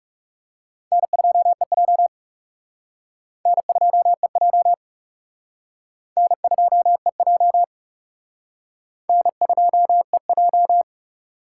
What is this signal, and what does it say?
Morse code, heard as a keyed tone, about 36 wpm, 700 Hz. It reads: N2EJ N2EJ N2EJ N2EJ
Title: Speed Racing - USA 1x2 Call Signs 30wpm